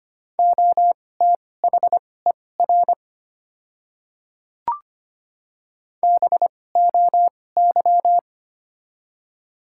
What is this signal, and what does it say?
Morse code, heard as a keyed tone, about 25 wpm, spaced as for 18 wpm, 700 Hz. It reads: OTHER BOY
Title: 2x Character Spacing - Top 200 Words - 25wpm